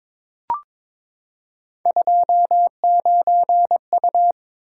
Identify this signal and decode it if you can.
Morse code, heard as a keyed tone, about 22 wpm, 700 Hz. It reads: 29U
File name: practice / 22wpm / Three Letter-Number Combinations - 22wpm